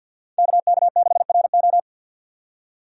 Morse code, heard as a keyed tone, about 50 wpm, 700 Hz. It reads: KK6RO
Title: USA Call Signs - 5 Characters - 50wpm